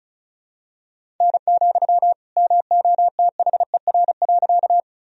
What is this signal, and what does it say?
Morse code, heard as a keyed tone, about 36 wpm, 700 Hz. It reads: N, MOTHER.
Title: Sentences from Top 200 Words and Abbreviations 35wpm